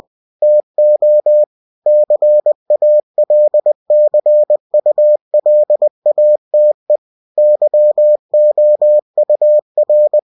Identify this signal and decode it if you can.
Morse code, heard as a keyed tone, about 20 wpm, 600 Hz. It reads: TO CALCULATE YOUR
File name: cw-books / winnie-the-pooh / winnie-20wpm-page-0117